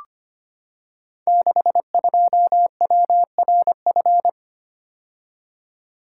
Morse code, heard as a keyed tone, about 25 wpm, 700 Hz. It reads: 62WRF
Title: Code Groups - 5 Letter:Numbers - 25wpm